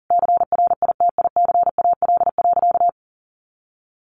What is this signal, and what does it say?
Morse code, heard as a keyed tone, about 40 wpm, 700 Hz. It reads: CRITICAL.
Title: Sentences from Top 1100 Words 40wpm